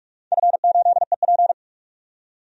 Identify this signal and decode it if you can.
Morse code, heard as a keyed tone, about 46 wpm, 700 Hz. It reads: F8EP
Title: DXCC-100 Call Signs - 4 Characters - 45wpm